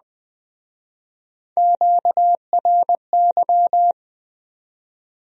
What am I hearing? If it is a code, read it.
Morse code, heard as a keyed tone, about 21 wpm, 700 Hz. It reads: QRY